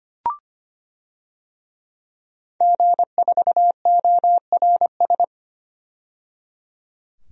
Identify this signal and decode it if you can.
Morse code, heard as a keyed tone, about 25 wpm, 700 Hz. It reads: G4ORS
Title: DXCC-100 Call Signs - 5 Characters - 25wpm